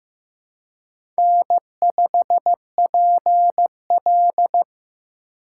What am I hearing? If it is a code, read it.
Morse code, heard as a keyed tone, about 15 wpm, 700 Hz. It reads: N5PL
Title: USA Call Signs - 4 Characters - 4x Character Spacing - 15wpm